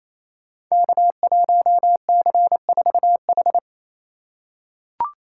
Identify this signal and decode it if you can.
Morse code, heard as a keyed tone, about 28 wpm, 700 Hz. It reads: K1C4H
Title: Code Groups - 5 Letter:Numbers - 28wpm